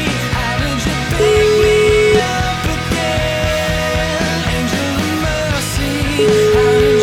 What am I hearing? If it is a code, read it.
Ringback tone (Europe, 425 Hz), with noise.